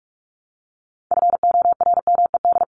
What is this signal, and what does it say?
Morse code, heard as a keyed tone, about 44 wpm, 700 Hz. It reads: FORMED
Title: ED Suffix - Top 100 Words with Base Words 45wpm